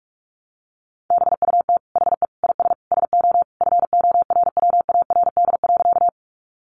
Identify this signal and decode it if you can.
Morse code, heard as a keyed tone, about 44 wpm, 700 Hz. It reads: BUT HE IS SO FORWARD.
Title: Sentences from Top 1300 Words 45wpm